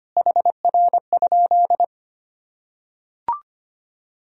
Morse code, heard as a keyed tone, about 25 wpm, 700 Hz. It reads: HR?
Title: Sentences from Top 300 Words and Abbreviations 25wpm